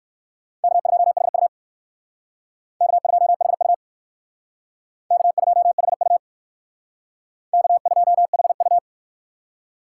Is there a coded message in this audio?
Yes: Morse code from 0.6 to 8.8 s, heard as a keyed tone, about 68 wpm, 700 Hz. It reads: K2HU K2HU K2HU K2HU